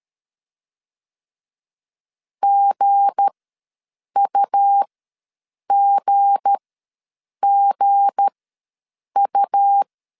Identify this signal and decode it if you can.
Morse code, heard as a keyed tone, about 13 wpm, spaced as for 7 wpm, 800 Hz. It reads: GUGGU
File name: LEZIONI / 7 GF9Z / 7 G ws 12wpm 3140